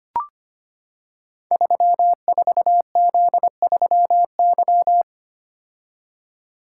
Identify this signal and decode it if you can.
Morse code, heard as a keyed tone, about 25 wpm, 700 Hz. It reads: 34Z3Y
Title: Code Groups - 5 Letter:Numbers - 25wpm